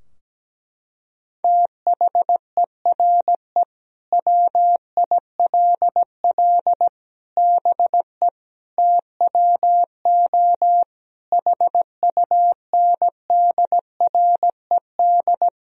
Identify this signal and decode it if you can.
Morse code, heard as a keyed tone, about 17 wpm, 700 Hz. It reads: THERE WILL BE TWO HUNDRED
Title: Sentences from Top 1000 Words - Review Part 3 - 17wpm